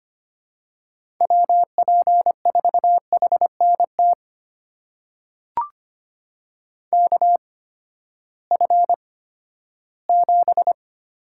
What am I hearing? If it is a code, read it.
Morse code, heard as a keyed tone, about 25 wpm, spaced as for 14 wpm, 700 Hz. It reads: WP4HNT KF7
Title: USA Call Signs - 6 Characters - 8x Character Spacing - 25wpm